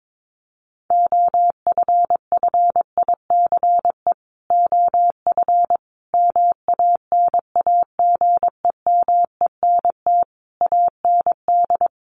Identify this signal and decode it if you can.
Morse code, heard as a keyed tone, about 22 wpm, 700 Hz. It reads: OFFICE OF MANAGEMENT AND
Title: Sets of 5 Words Encore2 22wpm